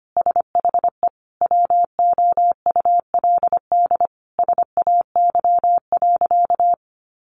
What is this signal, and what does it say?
Morse code, heard as a keyed tone, about 25 wpm, 700 Hz. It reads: SHE WOULD SAY.
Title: Sentences from Top 1500 Words 25wpm